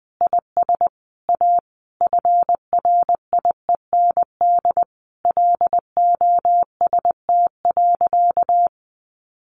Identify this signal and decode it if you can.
Morse code, heard as a keyed tone, about 20 wpm, 700 Hz. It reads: IS A FRIEND LOST.
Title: Sentences from Top 1400 Words 20wpm